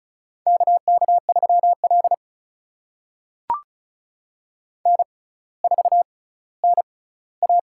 Morse code, heard as a keyed tone, about 36 wpm, spaced as for 18 wpm, 700 Hz. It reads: KK3L N4NA
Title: USA Call Signs - 4 Characters - 6x Character Spacing - 35wpm